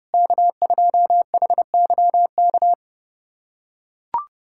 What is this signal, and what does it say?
Morse code, heard as a keyed tone, about 30 wpm, 700 Hz. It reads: K2HYK
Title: USA Callsign DE Callsign 30wpm